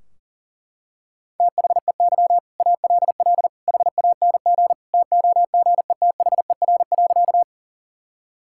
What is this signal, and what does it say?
Morse code, heard as a keyed tone, about 40 wpm, 700 Hz. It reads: THEY ALL HANG TOGETHER.
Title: Sentences from Top 1600 Words 40wpm